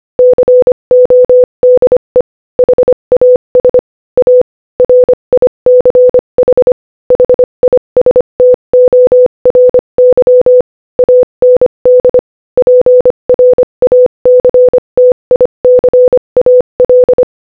Morse code, heard as a keyed tone, about 25 wpm, 500 Hz. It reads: CODE HAS A RICH HISTORY AND PRACTICAL
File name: Morse-Code-Audio-Speed-25